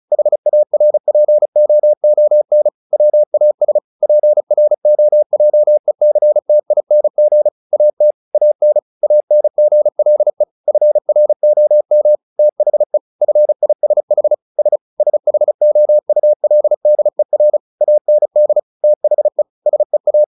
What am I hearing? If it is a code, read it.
Morse code, heard as a keyed tone, about 34 wpm, 600 Hz. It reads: HARPOON WAS PROJECTING AT AN ANGLE FROM THE FISH S SHOULDER AND THE SEA
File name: the old man and the sea 2 35wpm